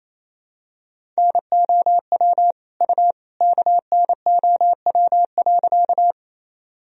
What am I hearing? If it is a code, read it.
Morse code, heard as a keyed tone, about 28 wpm, 700 Hz. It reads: NOW U KNOW.
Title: Sentences from Top 100 Words and Abbreviations 28wpm